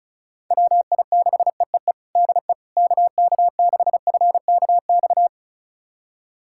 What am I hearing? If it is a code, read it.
Morse code, heard as a keyed tone, about 36 wpm, 700 Hz. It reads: WI6EEE DE KK6FKX
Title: USA Callsign DE Callsign - 6 Characters - 35wpm